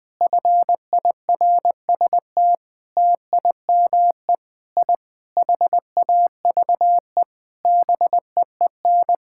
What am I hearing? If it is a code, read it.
Morse code, heard as a keyed tone, about 21 wpm, 700 Hz. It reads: FIRST TIME I HAVE BEEN
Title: Sentences with 6-Word Phrases - Spoken Beginning - 20wpm